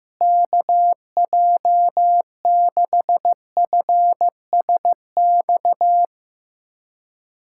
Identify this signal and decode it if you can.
Morse code, heard as a keyed tone, about 15 wpm, 700 Hz. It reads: KJ6FSX
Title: USA Callsign DE Callsign - 6 Characters - 15wpm